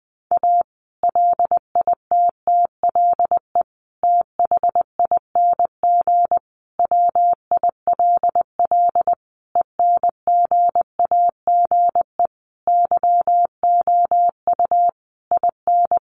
Morse code, heard as a keyed tone, about 20 wpm, 700 Hz. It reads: A LITTLE THING WILL ENGAGE YOU IN
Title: Sentences from Top 1500 Words 20wpm